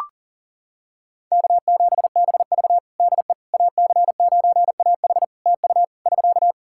Morse code, heard as a keyed tone, about 40 wpm, 700 Hz. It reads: K7BV DE AC9AH TU <SK>
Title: Practice QSO - Exchange 4 - Ending - 40wpm